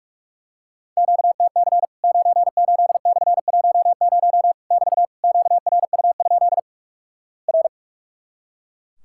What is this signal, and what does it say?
Morse code, heard as a keyed tone, about 44 wpm, 700 Hz. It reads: QTC 98/10 <BT> QRU? R
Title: Worked All Europe DX Contest - QTC Exchange - 45wpm